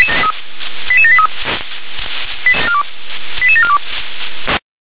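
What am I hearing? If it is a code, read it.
Five-tone selective call (ZVEI), heard as quick runs of five tones, tones of about 70 ms. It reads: 88783 88742
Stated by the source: ZVEI five-tone selective call received on SDR.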